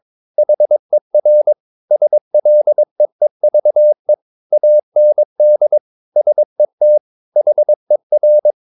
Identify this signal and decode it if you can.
Morse code, heard as a keyed tone, about 22 wpm, 600 Hz. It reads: HER SLEEVE AND SET HER
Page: Romanen Never af Ken Follet på engelsk.